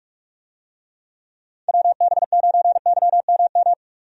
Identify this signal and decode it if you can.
Morse code, heard as a keyed tone, about 46 wpm, 700 Hz. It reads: WB9YMM
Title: USA Call Signs - 6 Characters - 45wpm